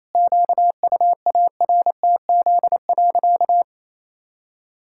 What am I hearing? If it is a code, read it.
Morse code, heard as a keyed tone, about 28 wpm, 700 Hz. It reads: QUARTZ.
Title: Pangram Challenge v2 - 28wpm